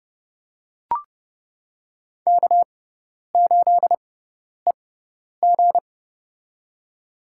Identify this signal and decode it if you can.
Morse code, heard as a keyed tone, about 30 wpm, spaced as for 10 wpm, 700 Hz. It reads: K8EG
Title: USA Call Signs - 4 Characters - 6x Character Spacing - 30wpm